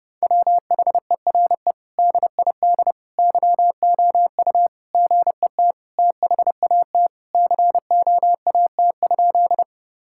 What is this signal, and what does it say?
Morse code, heard as a keyed tone, about 30 wpm, 700 Hz. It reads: WHERE DID YOU GET THAT COAT?
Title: Sentences from Top 1600 Words 30wpm